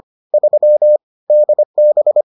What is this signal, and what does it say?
Morse code, heard as a keyed tone, about 25 wpm, 600 Hz. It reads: 3 DB